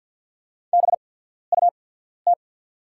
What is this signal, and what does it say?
Morse code, heard as a keyed tone, about 49 wpm, spaced as for 14 wpm, 690 Hz. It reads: BUT
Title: 8x Character Spacing - Top 100 Words - 50wpm